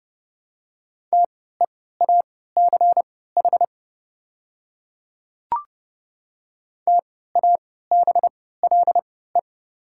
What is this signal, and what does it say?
Morse code, heard as a keyed tone, about 30 wpm, spaced as for 17 wpm, 700 Hz. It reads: TEACH TABLE
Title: 3x Character Spacing - Top 500 Words - 30wpm